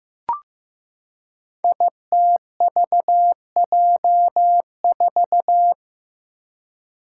Morse code, heard as a keyed tone, about 15 wpm, 700 Hz. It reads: ITVJ4
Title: Five Challenging Letter-Number Combinations 15wpm